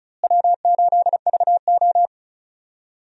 Morse code, heard as a keyed tone, about 36 wpm, 700 Hz. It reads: W8VO
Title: USA Call Signs - 4 Characters - Repeated 2x - 35wpm